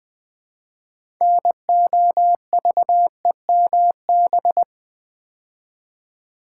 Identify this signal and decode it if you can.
Morse code, heard as a keyed tone, about 20 wpm, 700 Hz. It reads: NOVEMB
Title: Build A Word - 3 Letters At A Time - 8-Letter Words - 20wpm